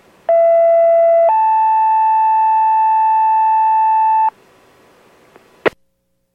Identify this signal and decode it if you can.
Two-tone page, heard as a one-second tone, then a three-second tone, about 670 then 880 Hz. It is heard from 0.3 to 4.3 s.